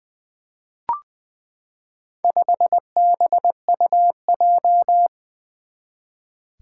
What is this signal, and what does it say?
Morse code, heard as a keyed tone, about 20 wpm, 700 Hz. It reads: 5BUJ
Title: Four Challenging Letter-Number Combinations 20wpm